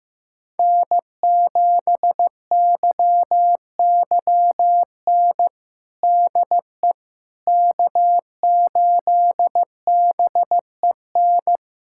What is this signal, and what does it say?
Morse code, heard as a keyed tone, about 16 wpm, 700 Hz. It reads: N7YYN DE K8BEN
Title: USA Callsign DE Callsign - 5 Characters - 15wpm